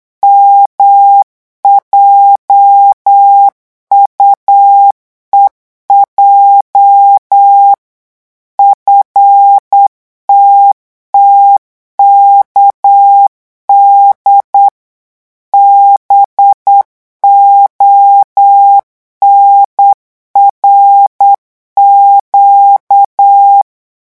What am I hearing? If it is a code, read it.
Morse code, heard as a keyed tone, about 8 wpm, 800 Hz. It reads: MJUEJ FTTKD BONRQ
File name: morse mp3 / revision8-vitesse 8 mots